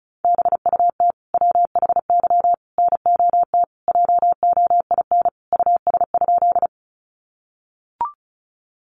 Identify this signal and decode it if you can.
Morse code, heard as a keyed tone, about 34 wpm, 700 Hz. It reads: BUT WHY NOT JOIN US?
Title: Sentences from Top 1400 Words 35wpm